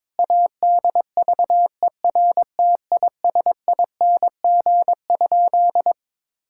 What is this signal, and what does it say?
Morse code, heard as a keyed tone, about 22 wpm, 700 Hz. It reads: ADVERTISING?
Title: Sentences from Top 1000 Words - Review Part 6 - 22wpm